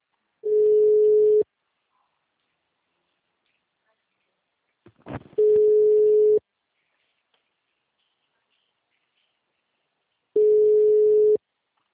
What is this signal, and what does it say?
Ringback tone (Europe, 425 Hz).